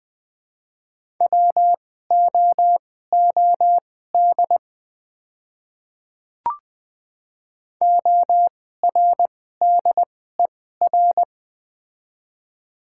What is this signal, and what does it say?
Morse code, heard as a keyed tone, about 20 wpm, spaced as for 14 wpm, 700 Hz. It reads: WOOD ORDER
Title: 2x Character Spacing - Top 500 Words - 20wpm